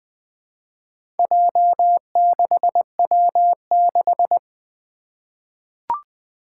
Morse code, heard as a keyed tone, about 20 wpm, 700 Hz. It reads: J6W6
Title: Four Challenging Letter-Number Combinations 20wpm